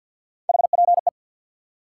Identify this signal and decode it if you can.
Morse code, heard as a keyed tone, about 50 wpm, 700 Hz. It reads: HPE
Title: Top 180 QSO Elements 50wpm